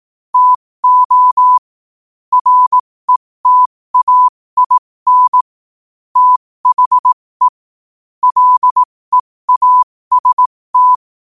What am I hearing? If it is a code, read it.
Morse code, heard as a keyed tone, about 18 wpm, spaced as for 15 wpm, 1000 Hz. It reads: TO RETAIN THE LEAST
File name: qotdfiles / 15wpm